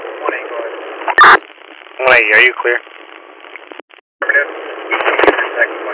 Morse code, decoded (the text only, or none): none